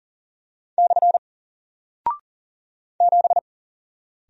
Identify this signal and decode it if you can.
Morse code, heard as a keyed tone, about 40 wpm, 700 Hz. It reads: / 7
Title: Single Character (Letter:Number:Punctuation:Prosign) 40wpm